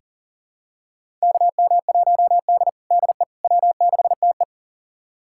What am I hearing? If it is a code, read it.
Morse code, heard as a keyed tone, about 40 wpm, 700 Hz. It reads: KM1D DE W6TE
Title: USA Callsign DE Callsign - 4 Characters - 40wpm